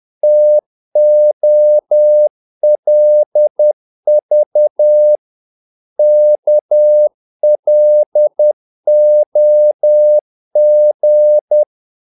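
Morse code, heard as a keyed tone, about 10 wpm, 600 Hz. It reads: TOLV KLOG
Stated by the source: Morse practice of short Danish words at 10 wpm, from CW med Gnister.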